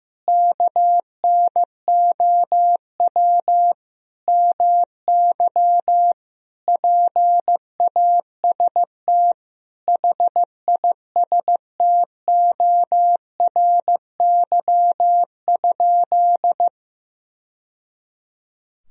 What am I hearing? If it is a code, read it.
Morse code, heard as a keyed tone, about 15 wpm, 700 Hz. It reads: KNOW MY PAST HISTORY?
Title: Sentences from Top 1000 Words - Review Part 3 - 15wpm